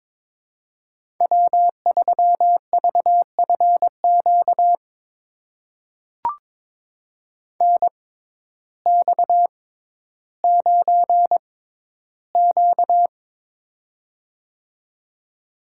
Morse code, heard as a keyed tone, about 22 wpm, spaced as for 12 wpm, 700 Hz. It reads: W3VFQ NX9Q
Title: USA Call Signs - 6x Character Spacing - 22wpm